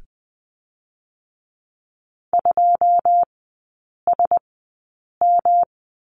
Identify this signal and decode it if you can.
Morse code, heard as a keyed tone, about 20 wpm, spaced as for 8 wpm, 700 Hz. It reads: 2SM
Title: 107 Lesson 18e - Letter m - Three Characters 20wpm